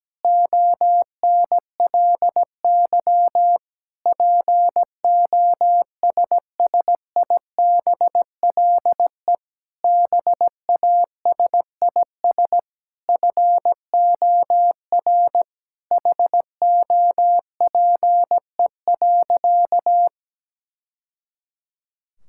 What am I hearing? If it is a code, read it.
Morse code, heard as a keyed tone, about 17 wpm, 700 Hz. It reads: ONLY POSSIBLE BASIS FOR HOPE.
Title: Sentences from Top 1000 Words - Review Part 5 - 17wpm